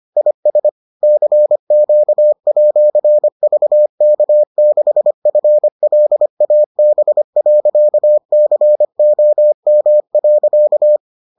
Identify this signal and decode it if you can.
Morse code, heard as a keyed tone, about 25 wpm, 600 Hz. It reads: IS CQ@VK6FLAB.COM.